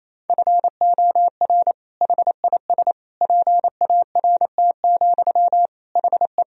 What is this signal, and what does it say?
Morse code, heard as a keyed tone, about 28 wpm, 700 Hz. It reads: FOR HIS PART, HE
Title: Sentences from Top 1000 Words - Review Part 2 - 28wpm